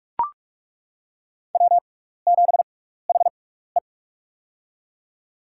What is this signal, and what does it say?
Morse code, heard as a keyed tone, about 42 wpm, spaced as for 15 wpm, 700 Hz. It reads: W7HE
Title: USA Call Signs - 4 Characters - 6x Character Spacing - 45wpm